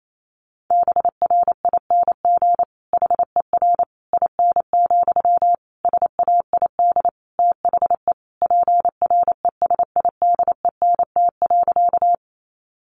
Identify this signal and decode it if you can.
Morse code, heard as a keyed tone, about 28 wpm, 700 Hz. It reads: BRING HER IN, SAID THE PRESIDENT.
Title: Sentences from Top 1000 Words 28wpm